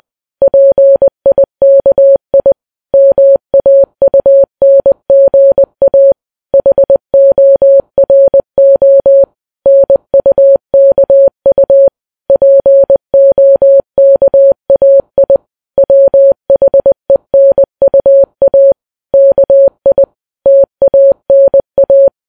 Morse code, heard as a keyed tone, about 20 wpm, 550 Hz. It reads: PIKI MAUNGA HORO NUKU POKAI WHENUA KI TANA